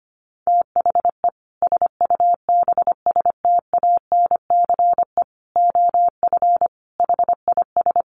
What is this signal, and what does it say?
Morse code, heard as a keyed tone, about 25 wpm, 700 Hz. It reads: THE SUBSTANCE OF HIS
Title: Sentences from Top 1500 Words 25wpm